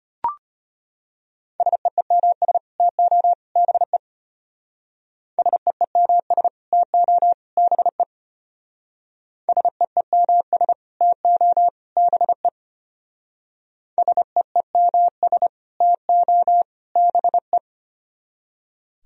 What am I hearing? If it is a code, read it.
Morse code, heard as a keyed tone, about 33 wpm, 700 Hz. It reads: SEEMS TO BE SEEMS TO BE SEEMS TO BE SEEMS TO BE